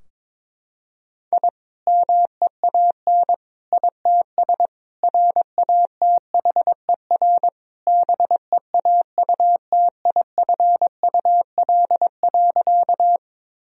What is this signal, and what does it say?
Morse code, heard as a keyed tone, about 22 wpm, 700 Hz. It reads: I MEAN ITS RATHER BEAUTIFUL.
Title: Sentences from Top 1000 Words - Review Part 4 - 22wpm